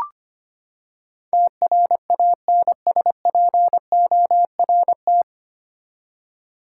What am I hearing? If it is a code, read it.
Morse code, heard as a keyed tone, about 25 wpm, 700 Hz. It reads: TRANSPORT
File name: Nine Letter Words - 25wpm